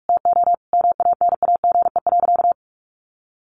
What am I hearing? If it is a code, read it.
Morse code, heard as a keyed tone, about 44 wpm, 700 Hz. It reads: TO MANAGE.
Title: Sentences from Top 1400 Words 45wpm